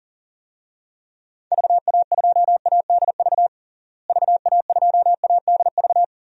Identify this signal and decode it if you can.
Morse code, heard as a keyed tone, about 40 wpm, 700 Hz. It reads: VA2ADV VA2ADV
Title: DXCC-100 Call Signs - 6 Characters - Repeated 3x - 40wpm